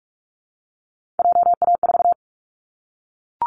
Morse code, heard as a keyed tone, about 44 wpm, 700 Hz. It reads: JA4
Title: Callsign Prefixes for Top 25 DXCC - Repeated 3x - 45wpm